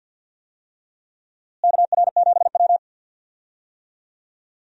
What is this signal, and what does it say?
Morse code, heard as a keyed tone, about 52 wpm, 700 Hz. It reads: KR7W
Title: USA Call Signs - 4 Characters - Repeated 3x - 50wpm